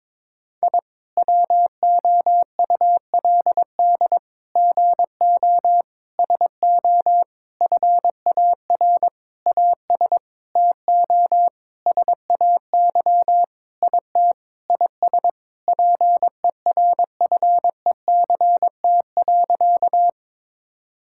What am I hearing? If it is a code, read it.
Morse code, heard as a keyed tone, about 22 wpm, 700 Hz. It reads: I WOULD GO SO FAR AS TO SAY IT IS PERFECT.
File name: practice / 22wpm / Sentences with 8-Word Phrases - 22wpm